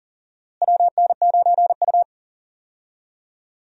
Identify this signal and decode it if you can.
Morse code, heard as a keyed tone, about 40 wpm, 700 Hz. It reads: WN9U
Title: USA Call Signs - 4 Characters - Repeated 2x - 40wpm